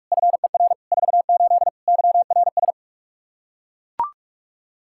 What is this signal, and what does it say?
Morse code, heard as a keyed tone, about 41 wpm, 700 Hz. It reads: FER 48 YRS
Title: K1USN SST QSO Practice 45wpm